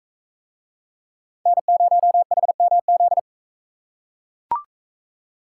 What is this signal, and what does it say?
Morse code, heard as a keyed tone, about 37 wpm, 700 Hz. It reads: N0HMZ